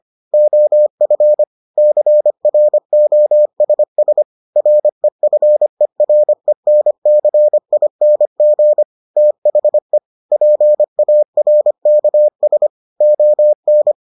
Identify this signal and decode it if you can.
Morse code, heard as a keyed tone, about 25 wpm, 600 Hz. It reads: OF CROSS REFERENCING THE PARKS ON